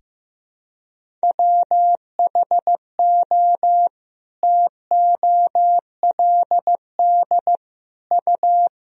Morse code, heard as a keyed tone, about 15 wpm, 700 Hz. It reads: WHO TOLD U
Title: Sentences from Top 400 Words and Abbreviations 15wpm